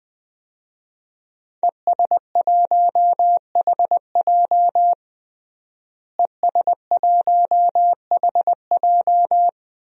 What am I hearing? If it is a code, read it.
Morse code, heard as a keyed tone, about 20 wpm, 700 Hz. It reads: ES1HJ ES1HJ
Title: DXCC-100 Call Signs - 5 Characters - Repeated 3x - 20wpm